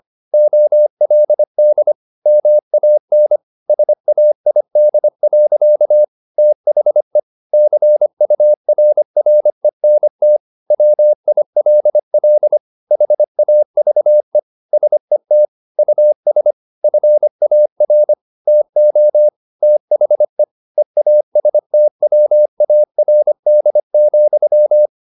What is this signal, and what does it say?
Morse code, heard as a keyed tone, about 25 wpm, 600 Hz. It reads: OLD MAN SAID. THE CURRENT WILL HAVE SET US FAR TO THE EASTWARD,
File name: the old man and the sea part 1 25wpm